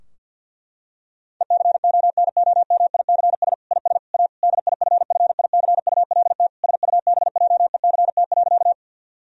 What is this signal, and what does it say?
Morse code, heard as a keyed tone, about 50 wpm, 700 Hz. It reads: ECONOMICS IS A DIFFICULT SUBJECT.
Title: Sentences from Top 1000 Words - Review Part 3 - 50wpm